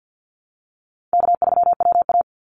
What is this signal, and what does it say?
Morse code, heard as a keyed tone, about 50 wpm, 700 Hz. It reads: K3WA
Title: Call Signs Repeated 3x - 50wpm